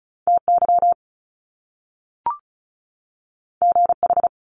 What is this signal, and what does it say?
Morse code, heard as a keyed tone, about 36 wpm, 700 Hz. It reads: TY GH
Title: Top Two Letter Combinations in Words 35wpm